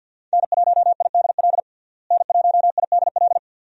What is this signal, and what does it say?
Morse code, heard as a keyed tone, about 52 wpm, 700 Hz. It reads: N1IDL N1IDL
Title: USA Call Signs - 5 Characters - Repeated 2x - 50wpm